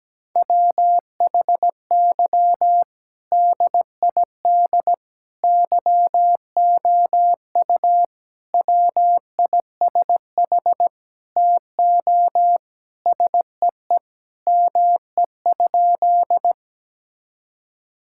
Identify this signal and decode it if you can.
Morse code, heard as a keyed tone, about 17 wpm, 700 Hz. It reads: WHY DID YOU WISH TO SEE ME?
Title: Sentences from Top 1600 Words 17wpm